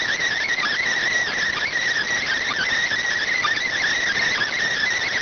Emergency Alert System (SAME) tones, heard as harsh buzzing bursts. It is heard from 0.5 to 4.5 s.